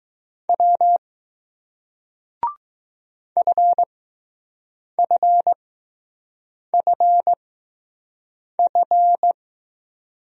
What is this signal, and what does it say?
Morse code, heard as a keyed tone, about 20 wpm, spaced as for 6 wpm, 700 Hz. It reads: W FFFF